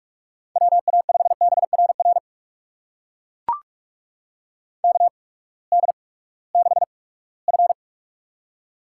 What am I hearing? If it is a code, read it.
Morse code, heard as a keyed tone, about 44 wpm, spaced as for 22 wpm, 690 Hz. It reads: WA5BRR KD6F
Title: USA Call Signs - 8x Character Spacing - 45wpm